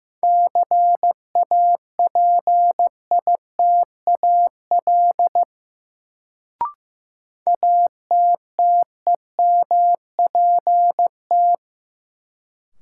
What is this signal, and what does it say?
Morse code, heard as a keyed tone, about 15 wpm, 700 Hz. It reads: CAPITAL ATTEMPT
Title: Sentences from Top 1000 Words - New Word Drill - 15wpm